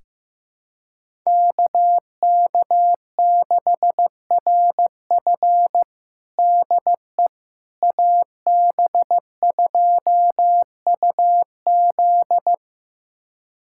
Morse code, heard as a keyed tone, about 15 wpm, 700 Hz. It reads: KK6RF DE AB2UZ
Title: USA Callsign DE Callsign - 5 Characters - 15wpm